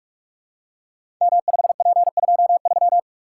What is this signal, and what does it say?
Morse code, heard as a keyed tone, about 44 wpm, 690 Hz. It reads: M5P23
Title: Code Groups - 5 Letter:Numbers - 45wpm